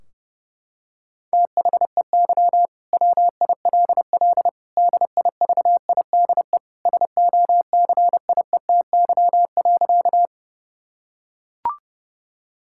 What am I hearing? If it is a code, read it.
Morse code, heard as a keyed tone, about 30 wpm, 700 Hz. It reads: THEY WILL DIVIDE SOCIETY.
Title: Sentences from Top 1600 Words 30wpm